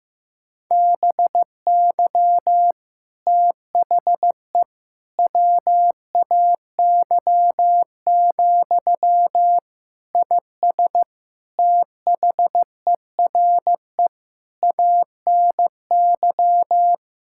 Morse code, heard as a keyed tone, about 15 wpm, 700 Hz. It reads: BY THE WAY, IS THERE ANY
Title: Sentences from Top 1600 Words 15wpm